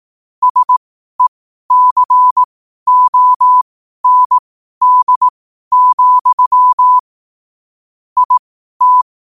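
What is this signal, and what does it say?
Morse code, heard as a keyed tone, about 18 wpm, spaced as for 12 wpm, 1000 Hz. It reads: SECOND, IT
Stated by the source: Morse code quote of the day at 12 wpm.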